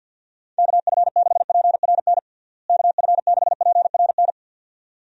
Morse code, heard as a keyed tone, about 50 wpm, 700 Hz. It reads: KF6PRN KF6PRN
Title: USA Call Signs - 6 Characters - Repeated 2x - 50wpm